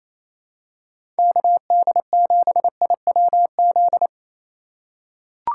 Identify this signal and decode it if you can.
Morse code, heard as a keyed tone, about 28 wpm, 700 Hz. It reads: KD7IWZ
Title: USA Call Signs - 6 Characters - 3x Character Spacing - 28wpm